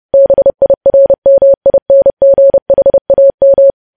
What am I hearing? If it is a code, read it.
Morse code, heard as a keyed tone, about 30 wpm, 550 Hz. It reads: BIRMINGHAM
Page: news headlines in morse code at 30 wpm